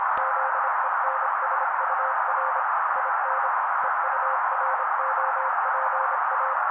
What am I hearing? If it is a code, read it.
Morse code, heard as a keyed tone, about 27 wpm, 530 Hz. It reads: ZENSUR IN EUROPA
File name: daily / 25wpm noise